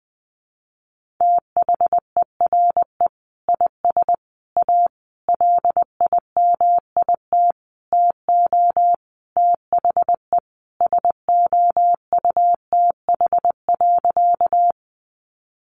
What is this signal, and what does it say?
Morse code, heard as a keyed tone, about 20 wpm, 700 Hz. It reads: THERE IS A LIMIT TO THE SOUTH.
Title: Sentences from Top 1500 Words 20wpm